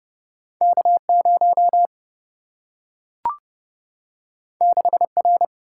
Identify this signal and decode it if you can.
Morse code, heard as a keyed tone, about 30 wpm, 700 Hz. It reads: K 0 6 R
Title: Single-Letter Single-Number 30wpm